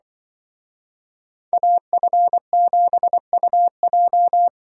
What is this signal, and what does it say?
Morse code, heard as a keyed tone, about 24 wpm, 700 Hz. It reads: AF7UJ
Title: Speed Racing - USA 2x2 Call Signs 20wpm